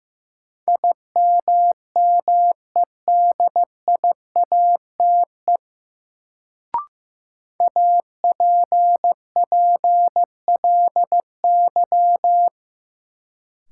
Morse code, heard as a keyed tone, about 15 wpm, 700 Hz. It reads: IMMEDIATE APPLY
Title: Sentences from Top 1200 Words - New Word Drill Alt - 15wpm